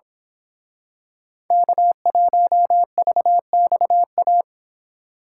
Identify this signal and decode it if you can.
Morse code, heard as a keyed tone, about 26 wpm, 700 Hz. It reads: K1VXA
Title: Speed Racing - USA 1x3 Call Signs 22wpm